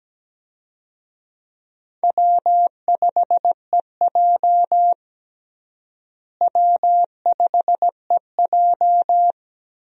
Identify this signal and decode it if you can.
Morse code, heard as a keyed tone, about 17 wpm, 700 Hz. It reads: W5EJ W5EJ
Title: USA Call Signs - 4 Characters - Repeated 3x - 17wpm